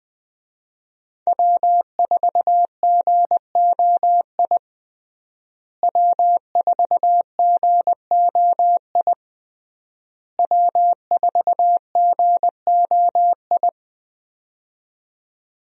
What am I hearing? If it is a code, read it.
Morse code, heard as a keyed tone, about 20 wpm, 700 Hz. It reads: W4GOI W4GOI W4GOI
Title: USA Call Signs - 5 Characters - Repeated 3x - 20wpm